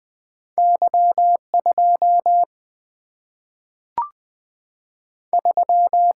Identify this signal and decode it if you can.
Morse code, heard as a keyed tone, about 20 wpm, 700 Hz. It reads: Y2 3
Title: Single-Letter Single-Number 20wpm